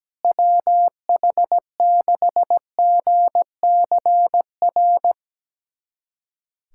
Morse code, heard as a keyed tone, about 17 wpm, 700 Hz. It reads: WH6GCR
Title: USA Callsign DE Callsign - 6 Characters - 17wpm